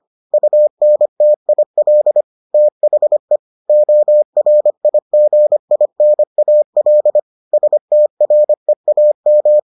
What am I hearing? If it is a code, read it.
Morse code, heard as a keyed tone, about 25 wpm, 600 Hz. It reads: UNTIL THE ORIGINAL STREAM